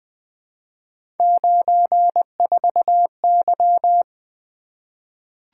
Morse code, heard as a keyed tone, about 20 wpm, 700 Hz. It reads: 94Y
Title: Three Letter-Number Combinations - 20wpm